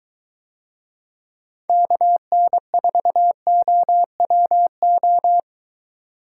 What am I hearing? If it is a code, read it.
Morse code, heard as a keyed tone, about 22 wpm, 700 Hz. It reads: KN4OWO